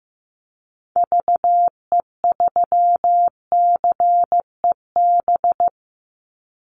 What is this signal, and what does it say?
Morse code, heard as a keyed tone, about 15 wpm, 700 Hz. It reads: VE3CEB
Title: Call Signs Repeated 2x - 15wpm